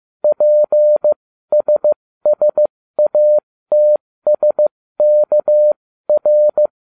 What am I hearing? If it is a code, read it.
Morse code, heard as a keyed tone, about 15 wpm, spaced as for 13 wpm, 600 Hz. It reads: PSSATSKR